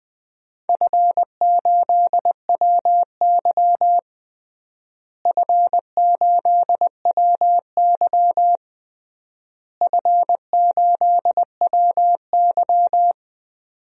Morse code, heard as a keyed tone, about 20 wpm, 700 Hz. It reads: F8WY F8WY F8WY
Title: DXCC-100 Call Signs - 4 Characters - Repeated 3x - 20wpm